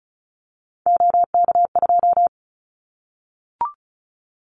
Morse code, heard as a keyed tone, about 36 wpm, 700 Hz. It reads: OK2
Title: Callsign Prefixes for Top 50 DXCC - 35wpm